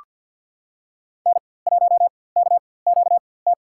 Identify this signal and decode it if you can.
Morse code, heard as a keyed tone, about 47 wpm, spaced as for 23 wpm, 700 Hz. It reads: N1KQT
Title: USA Call Signs - 4x Character Spacing - 50wpm